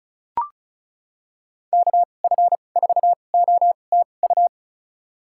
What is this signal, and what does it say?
Morse code, heard as a keyed tone, about 36 wpm, spaced as for 25 wpm, 700 Hz. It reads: KF4OTU
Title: USA Call Signs - 6 Characters - 2x Character Spacing - 35wpm